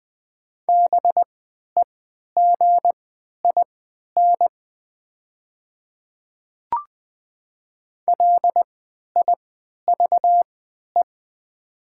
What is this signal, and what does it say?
Morse code, heard as a keyed tone, about 20 wpm, spaced as for 11 wpm, 700 Hz. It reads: BEGIN LIVE
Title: 3x Character Spacing - Top 300 Words - 20wpm